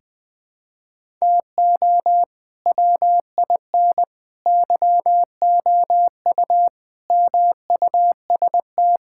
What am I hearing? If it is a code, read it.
Morse code, heard as a keyed tone, about 20 wpm, 700 Hz. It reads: TO WIN YOU MUST
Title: Sentences with 9-Word Phrases - Spoken Beginning - 20wpm